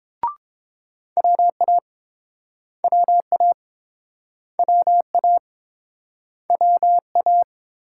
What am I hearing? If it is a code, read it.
Morse code, heard as a keyed tone, about 28 wpm, 700 Hz. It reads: WA WA WA WA